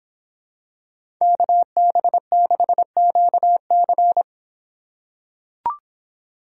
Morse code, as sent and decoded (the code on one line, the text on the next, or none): -.- -... -.... --.- -.-.
KB6QC